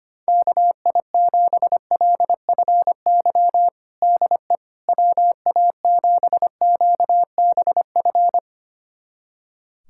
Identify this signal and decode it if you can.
Morse code, heard as a keyed tone, about 25 wpm, 700 Hz. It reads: KI7LFY DE WA7QBF
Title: USA Callsign DE Callsign 25wpm